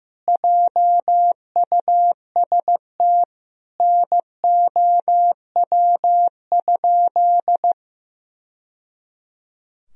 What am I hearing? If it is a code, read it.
Morse code, heard as a keyed tone, about 15 wpm, 700 Hz. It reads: JUST NOW?
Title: Sentences from Top 1000 Words - Review Part 3 - 15wpm